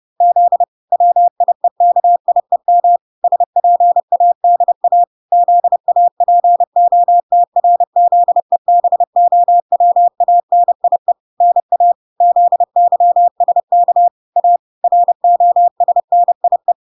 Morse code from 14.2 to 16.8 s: A ROSNIE